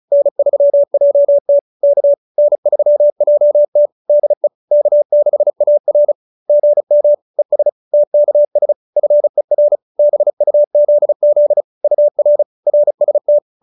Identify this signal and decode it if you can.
Morse code, heard as a keyed tone, about 34 wpm, 590 Hz. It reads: N3JT K N3JT DE K6AR GM ES TKS FER BUZZ UR RST